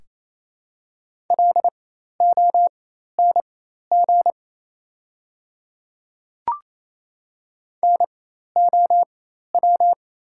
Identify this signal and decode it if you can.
Morse code, heard as a keyed tone, about 28 wpm, spaced as for 13 wpm, 700 Hz. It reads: LONG NOW
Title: 4x Character Spacing - Top 100 Words - 28wpm